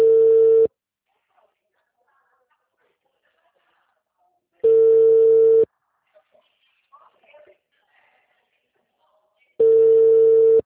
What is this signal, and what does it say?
Ringback tone (China, 450 Hz), with speech.
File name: out- -2013- - - .67915